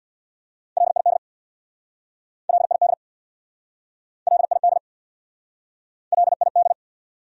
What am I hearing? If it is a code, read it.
Morse code, heard as a keyed tone, about 69 wpm, 700 Hz. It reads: LID LID LID LID